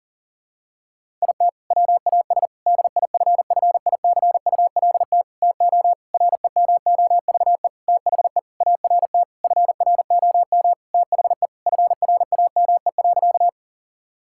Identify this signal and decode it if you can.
Morse code, heard as a keyed tone, about 40 wpm, 700 Hz. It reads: IT WAS DIFFICULT TO REMOVE THE ART FROM THE FRAME.
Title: Sentences from Top 1000 Words - Review Part 6 - 40wpm